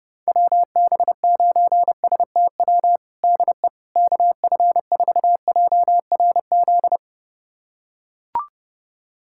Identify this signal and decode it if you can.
Morse code, heard as a keyed tone, about 32 wpm, 700 Hz. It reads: WB9STW DE KF4JRZ
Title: USA Callsign DE Callsign 30wpm